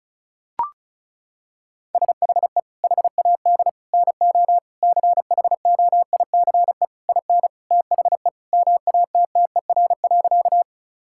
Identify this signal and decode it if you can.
Morse code, heard as a keyed tone, about 36 wpm, 700 Hz. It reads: SHE HAD NO CHOICE IN THE MATTER.
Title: Sentences from Top 1000 Words - Review Part 6 - 35wpm